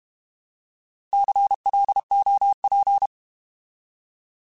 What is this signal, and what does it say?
Morse code, heard as a keyed tone, about 32 wpm, 800 Hz. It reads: CLOP
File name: Alphabetical / Alphabetical mp3s / Cs @30